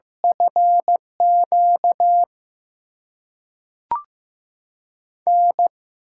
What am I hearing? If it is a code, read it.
Morse code, heard as a keyed tone, about 15 wpm, 700 Hz. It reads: FQ N
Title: USA Call Signs - 6x Character Spacing - 15wpm